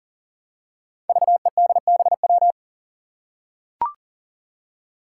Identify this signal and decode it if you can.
Morse code, heard as a keyed tone, about 40 wpm, 700 Hz. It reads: VEDBW
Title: Five Challenging Letter-Number Combinations 40wpm